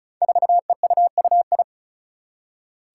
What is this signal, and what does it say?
Morse code, heard as a keyed tone, about 36 wpm, 700 Hz. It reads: 4EUUI
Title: Five Challenging Letter-Number Combinations 35wpm